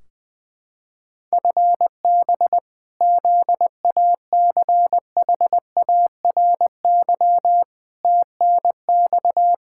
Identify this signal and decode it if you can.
Morse code, heard as a keyed tone, about 20 wpm, 700 Hz. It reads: FB ZACHARY TNX
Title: Practice QSO - Exchange 3 - More Info - 20wpm